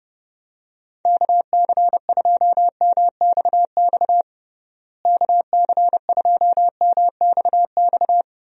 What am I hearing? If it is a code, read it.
Morse code, heard as a keyed tone, about 30 wpm, 700 Hz. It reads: KC2MXX KC2MXX
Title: USA Call Signs - 6 Characters - Repeated 3x - 30wpm